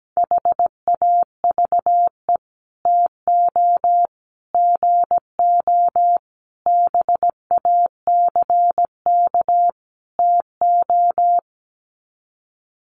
Morse code, heard as a keyed tone, about 17 wpm, 700 Hz. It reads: HAVE TO GO BACK TO
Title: Sets of 5 Words Encore2 17wpm